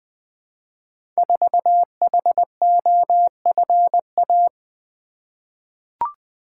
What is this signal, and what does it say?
Morse code, heard as a keyed tone, about 20 wpm, 700 Hz. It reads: 4HOFA
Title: Code Groups - 5 Letter:Numbers - 20wpm